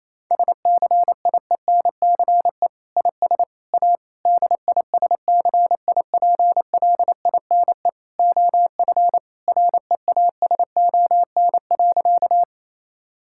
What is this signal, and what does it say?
Morse code, heard as a keyed tone, about 28 wpm, 700 Hz. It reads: SCIENCE IS A DISCIPLINE OF REASON.
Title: Sentences from Top 1000 Words - Review Part 6 - 28wpm